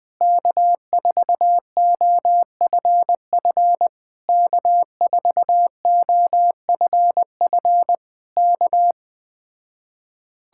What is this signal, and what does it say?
Morse code, heard as a keyed tone, about 20 wpm, 700 Hz. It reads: K4OFF K4OFF K
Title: Speed-Racing Common QSO Phrases Encore - 17wpm